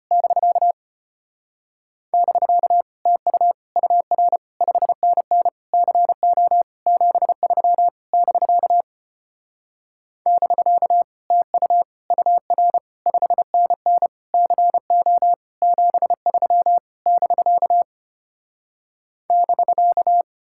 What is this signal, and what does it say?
Morse code, heard as a keyed tone, about 33 wpm, 700 Hz. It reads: <BK> <BK> TU UR 5NN CO 73 <BK> <BK> TU UR 5NN CO 73 <BK> <BK>